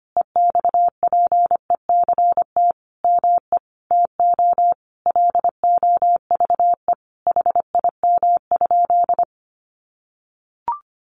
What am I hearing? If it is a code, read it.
Morse code, heard as a keyed tone, about 25 wpm, 700 Hz. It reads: EXPECT ME TO LOVE HIM?
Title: Sentences from Top 1100 Words 25wpm